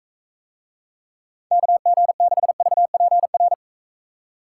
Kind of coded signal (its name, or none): Morse code